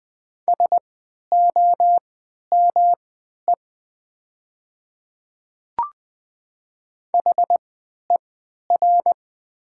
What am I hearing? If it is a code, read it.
Morse code, heard as a keyed tone, about 20 wpm, spaced as for 11 wpm, 700 Hz. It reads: SOME HER
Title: 3x Character Spacing - Top 100 Words - 20wpm